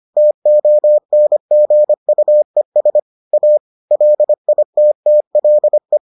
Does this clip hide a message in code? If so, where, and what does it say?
Yes: Morse code from 0.2 to 6.0 s, heard as a keyed tone, about 25 wpm, 600 Hz. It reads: TONGUES A LITTLE